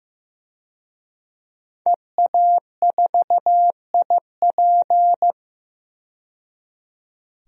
Morse code, heard as a keyed tone, about 15 wpm, 700 Hz. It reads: EA4IP
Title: DXCC-100 Call Signs - 5 Characters - 15wpm